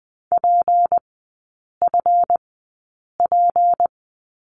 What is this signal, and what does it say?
Morse code, heard as a keyed tone, about 20 wpm, spaced as for 8 wpm, 700 Hz. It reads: PFP
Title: 184 Lesson 30e - Letter b - Three Characters 20wpm